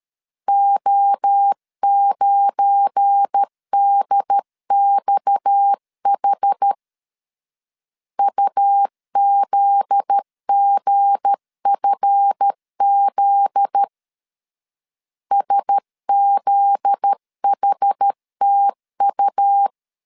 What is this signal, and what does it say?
Morse code, heard as a keyed tone, about 13 wpm, 800 Hz. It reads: O9DXH UZGFZ SZHTU
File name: LEZIONI / 7 GF9Z / 7 GF9Z 12wpm 2147